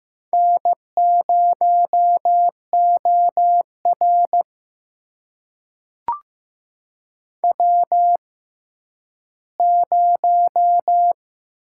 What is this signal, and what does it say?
Morse code, heard as a keyed tone, about 15 wpm, spaced as for 10 wpm, 700 Hz. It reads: N0OR W0
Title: USA Call Signs - 4 Characters - 6x Character Spacing - 15wpm